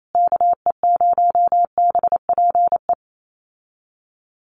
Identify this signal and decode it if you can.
Morse code, heard as a keyed tone, about 28 wpm, 700 Hz. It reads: KE0BPE
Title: USA Call Signs Repeated 3x 28wpm